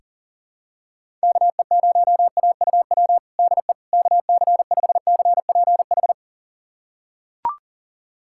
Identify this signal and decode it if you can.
Morse code, heard as a keyed tone, about 37 wpm, 700 Hz. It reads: KE0AUW DE KC5CPH
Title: USA Callsign DE Callsign - 6 Characters - 40wpm